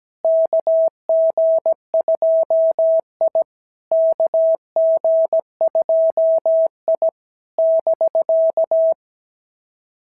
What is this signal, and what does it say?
Morse code, heard as a keyed tone, about 16 wpm, 650 Hz. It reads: KG2I KG2I <BK>
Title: SOTA QSO Practice 17wpm